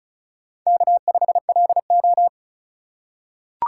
Morse code, heard as a keyed tone, about 34 wpm, 700 Hz. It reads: K5LO
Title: USA Call Signs - 4 Characters - 3x Character Spacing - 35wpm